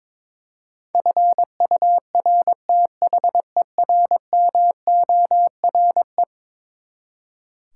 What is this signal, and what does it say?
Morse code, heard as a keyed tone, about 22 wpm, 700 Hz. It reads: FURTHERMORE
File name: practice / 22wpm / Eleven Letter Words - 22wpm